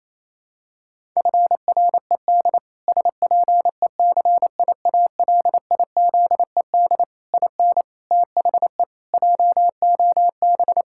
Morse code, heard as a keyed tone, about 28 wpm, 700 Hz. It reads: FRED SPECIALIZED IN THE JOB
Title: Pangram Challenge v2 - 28wpm